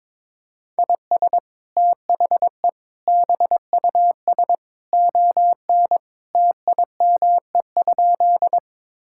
Morse code, heard as a keyed tone, about 22 wpm, 700 Hz. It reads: IS THE BUS ON TIME?
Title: Sentences from Top 1000 Words - Review Part 6 - 22wpm